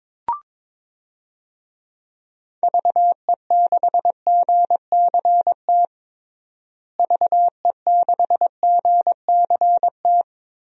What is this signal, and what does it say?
Morse code, heard as a keyed tone, about 22 wpm, 700 Hz. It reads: VE6GCT VE6GCT
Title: DXCC-100 Call Signs - 6 Characters - Repeated 2x - 22wpm